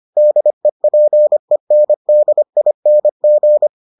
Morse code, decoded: DEPENDING

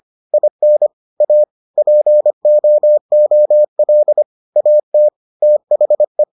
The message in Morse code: IN A POOL AT THE

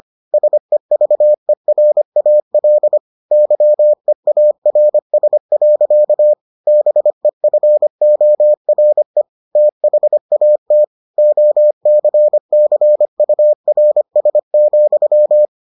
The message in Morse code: SEVERAL YEARS. BEFORE THAT OCCURS,

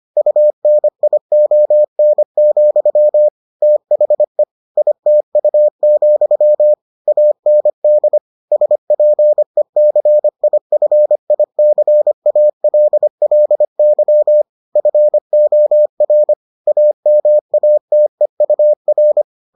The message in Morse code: UNION, THE ITU, AND SPECIFICALLY FOR AMATEUR

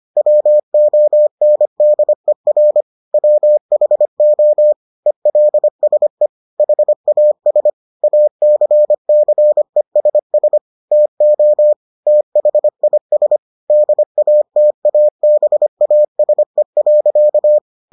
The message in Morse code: WONDER WHO ELSE HAS ACCESS TO THIS DATABASE.

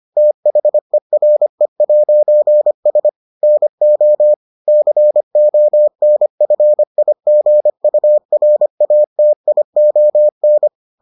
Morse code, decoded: THERE'S NO CONFIGURATION

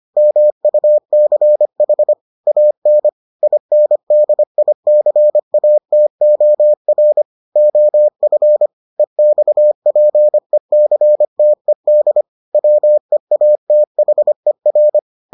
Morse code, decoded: MUCH AN INDICATOR OF EXPECTED WEATHER

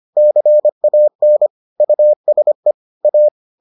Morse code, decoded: CAN USE A

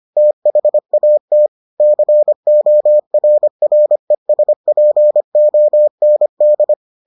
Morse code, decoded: THAT CORRESPOND

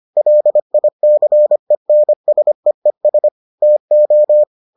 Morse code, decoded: LICENSEES TO

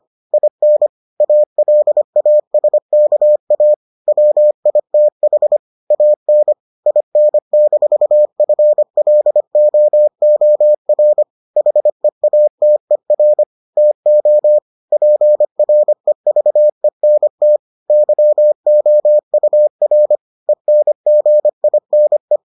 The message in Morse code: IN ALASKA WITH AN IN-FLOOR HEATER TO PREVENT YOUR ENGINE

